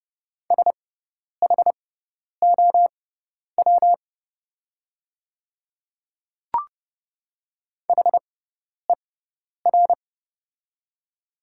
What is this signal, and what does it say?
Morse code, heard as a keyed tone, about 29 wpm, spaced as for 10 wpm, 690 Hz. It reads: SHOW HER
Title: 6x Character Spacing - Top 300 Words - 30wpm